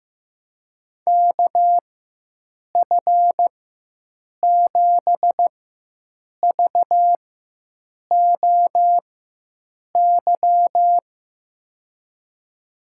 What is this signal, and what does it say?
Morse code, heard as a keyed tone, about 16 wpm, spaced as for 7 wpm, 700 Hz. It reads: KF7VOY
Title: USA Call Signs - 4x Character Spacing - 15wpm